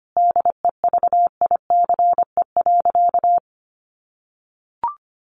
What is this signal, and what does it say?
Morse code, heard as a keyed tone, about 25 wpm, 700 Hz. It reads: DEVICE.
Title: Sentences from Top 1500 Words 25wpm